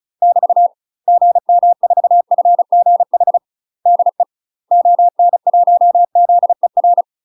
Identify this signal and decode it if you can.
Morse code, heard as a keyed tone, about 34 wpm, 700 Hz. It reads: <BT> GM4FGH DE ON1ZER
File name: UK MASTER QSOs TRAINING / 35WPM / UK MASTER QSOs 0003